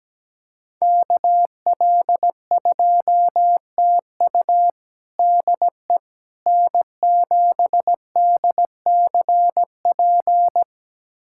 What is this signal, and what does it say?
Morse code, heard as a keyed tone, about 17 wpm, 700 Hz. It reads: KL2TU DE N7DCP
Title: USA Callsign DE Callsign - 5 Characters - 17wpm